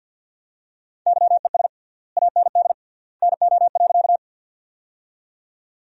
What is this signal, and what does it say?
Morse code, heard as a keyed tone, about 47 wpm, 700 Hz. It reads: YES AND NO.
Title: 3x Word-Spacing - Sentences from Top 500 Words - 50wpm